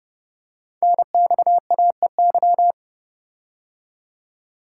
Morse code, heard as a keyed tone, about 30 wpm, 700 Hz. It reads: NXAEY
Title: Code Groups - 5 Letters - 30wpm